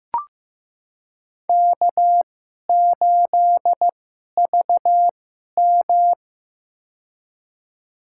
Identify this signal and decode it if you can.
Morse code, heard as a keyed tone, about 15 wpm, spaced as for 11 wpm, 700 Hz. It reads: K8VM
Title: USA Call Signs - 2x Character Spacing - 15wpm